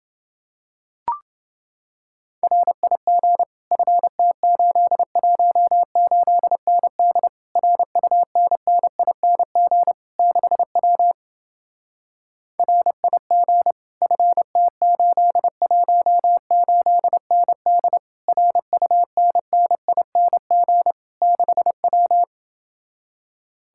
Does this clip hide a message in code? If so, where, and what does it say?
Yes: Morse code from 2.4 to 22.2 s, heard as a keyed tone, about 29 wpm, 700 Hz. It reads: RIG FT818ND RUNNING 6W RIG FT818ND RUNNING 6W